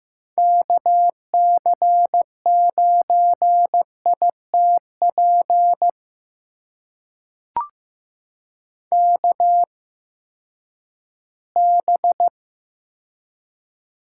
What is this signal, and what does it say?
Morse code, heard as a keyed tone, about 15 wpm, spaced as for 11 wpm, 700 Hz. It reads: KC9ITP KB
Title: USA Call Signs - 6 Characters - 8x Character Spacing - 15wpm